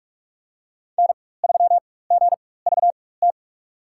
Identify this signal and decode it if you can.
Morse code, heard as a keyed tone, about 44 wpm, spaced as for 21 wpm, 690 Hz. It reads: N3GVT
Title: USA Call Signs - 5 Characters - 4x Character Spacing - 45wpm